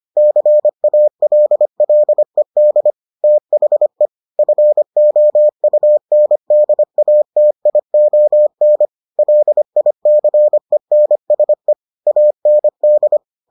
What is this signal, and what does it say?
Morse code, heard as a keyed tone, about 25 wpm, 600 Hz. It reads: CALLED THE FOUNDATION LICENSE AND